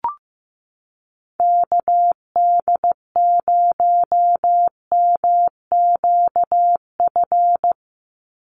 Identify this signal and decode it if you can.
Morse code, heard as a keyed tone, about 15 wpm, 700 Hz. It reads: KD0MQF
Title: USA Call Signs 15wpm